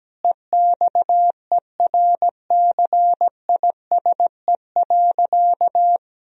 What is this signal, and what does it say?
Morse code, heard as a keyed tone, about 17 wpm, 700 Hz. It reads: EXERCISE.
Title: Sentences from Top 1000 Words - Review Part 4 - 17wpm